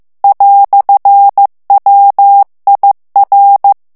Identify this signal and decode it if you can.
Morse code, heard as a keyed tone, about 15 wpm, 800 Hz. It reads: "WIR
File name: morse mp3 / ALLEMAND texte clair 1 vitesse 14 mots minute